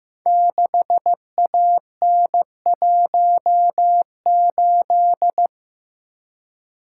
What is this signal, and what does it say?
Morse code, heard as a keyed tone, about 15 wpm, 700 Hz. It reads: 6AN18
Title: Code Groups - 5 Letter:Numbers - 15wpm